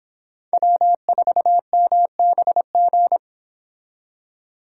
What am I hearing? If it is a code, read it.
Morse code, heard as a keyed tone, about 25 wpm, 700 Hz. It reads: W4MBG